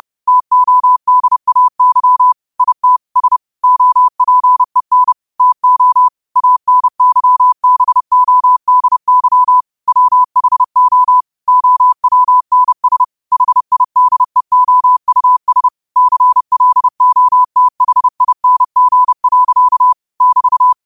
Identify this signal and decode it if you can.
Morse code, heard as a keyed tone, about 30 wpm, 1000 Hz. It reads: TODAY IT S OPEN TO ANYBODY WHO OWNS HIDEOUS CLOTHING. <BT>
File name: qotdfiles / 30wpm